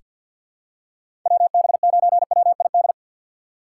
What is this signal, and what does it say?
Morse code, heard as a keyed tone, about 47 wpm, 700 Hz. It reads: WB9WID
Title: USA Call Signs - 4x Character Spacing - 50wpm